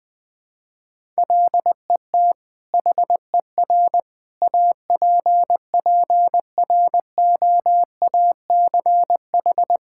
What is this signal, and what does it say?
Morse code, heard as a keyed tone, about 20 wpm, 700 Hz. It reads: LET HER APPROACH
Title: Sentences from Top 1000 Words - Review Part 6 - 20wpm